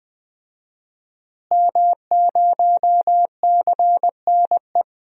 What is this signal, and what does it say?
Morse code, heard as a keyed tone, about 20 wpm, 700 Hz. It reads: M0CNE
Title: DXCC-100 Call Signs - 5 Characters - Repeated 3x - 20wpm